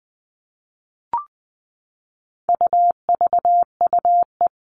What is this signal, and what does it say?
Morse code, heard as a keyed tone, about 20 wpm, 700 Hz. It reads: UVUE
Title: All Cut Numbers - 4-Digit 20wpm